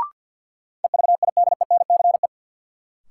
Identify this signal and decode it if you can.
Morse code, heard as a keyed tone, about 52 wpm, 700 Hz. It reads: EVIDENCE
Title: Sets of 5 Words Encore - New Word Drill - 50wpm